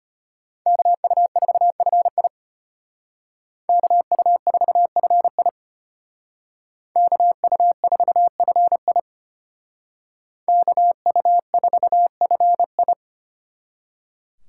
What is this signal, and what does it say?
Morse code, heard as a keyed tone, about 32 wpm, 700 Hz. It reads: KU4FI KU4FI KU4FI KU4FI